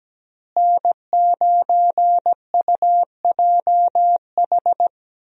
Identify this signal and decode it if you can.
Morse code, heard as a keyed tone, about 17 wpm, 700 Hz. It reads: N9UJH
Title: USA Call Signs - 5 Characters - Repeated 3x - 17wpm